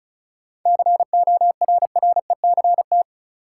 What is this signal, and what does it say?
Morse code, heard as a keyed tone, about 36 wpm, 700 Hz. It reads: CORRECT
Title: Sentences from Top 700 Words - New Word Drill - 35wpm